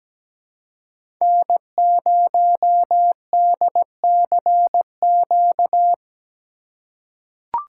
Morse code, heard as a keyed tone, about 17 wpm, 700 Hz. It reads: N0DCQ
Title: USA Call Signs - 5 Characters - 4x Character Spacing - 17wpm